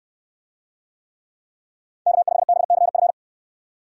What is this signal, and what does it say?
Morse code, heard as a keyed tone, about 69 wpm, 700 Hz. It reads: K5BCF